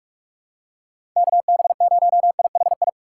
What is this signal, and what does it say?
Morse code, heard as a keyed tone, about 46 wpm, 700 Hz. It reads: KB0IHI
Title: USA Call Signs - 6 Characters - 6x Character Spacing - 45wpm